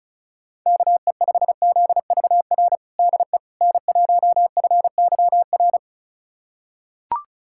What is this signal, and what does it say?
Morse code, heard as a keyed tone, about 34 wpm, 700 Hz. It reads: KE5ZVR DE N1FYR
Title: USA Callsign DE Callsign 35wpm